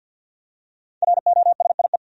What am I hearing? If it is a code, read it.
Morse code, heard as a keyed tone, about 51 wpm, 700 Hz. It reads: ROSIE
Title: USA Common Names - 50wpm